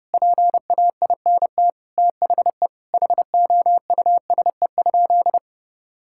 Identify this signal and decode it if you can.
Morse code, heard as a keyed tone, about 29 wpm, 700 Hz. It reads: PAINT THE HOUSE?
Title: Sentences from Top 1000 Words - Review Part 6 - 30wpm